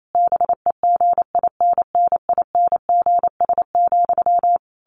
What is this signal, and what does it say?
Morse code, heard as a keyed tone, about 28 wpm, 700 Hz. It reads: BEGINNINGS,
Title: A Sound Of Thunder 28wpm